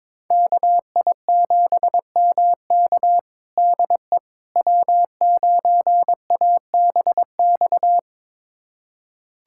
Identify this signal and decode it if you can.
Morse code, heard as a keyed tone, about 22 wpm, 700 Hz. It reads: KI7MK DE W9ABX
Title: USA Callsign DE Callsign - 5 Characters - 22wpm